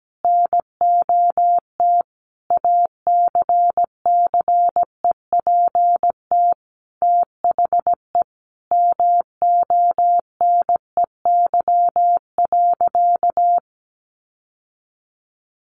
Morse code, heard as a keyed tone, about 17 wpm, 700 Hz. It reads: NOT ACCEPT THE MONEY.
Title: Sentences from Top 1300 Words 17wpm